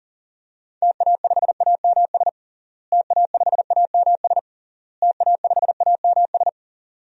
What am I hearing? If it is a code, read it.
Morse code, heard as a keyed tone, about 40 wpm, 700 Hz. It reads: TA5AMS TA5AMS TA5AMS
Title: DXCC-100 Call Signs - 6 Characters - Repeated 3x - 40wpm